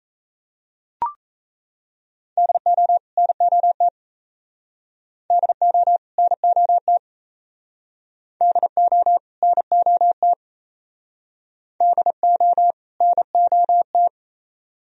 Morse code, heard as a keyed tone, about 34 wpm, 700 Hz. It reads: DO NOT DO NOT DO NOT DO NOT
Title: Speed-Racing - Sets of 2 Words 28wpm